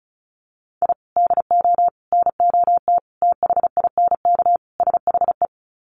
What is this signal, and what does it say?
Morse code, heard as a keyed tone, about 34 wpm, 700 Hz. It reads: I DO NOT THINK SHE
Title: Sets of 5 Words 35wpm